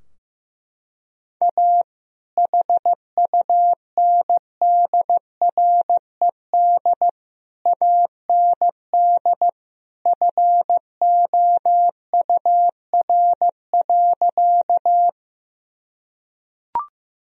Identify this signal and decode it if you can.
Morse code, heard as a keyed tone, about 16 wpm, 700 Hz. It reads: A HUNDRED AND FOUR.
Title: Sentences from Top 1000 Words - Review Part 2 - 15wpm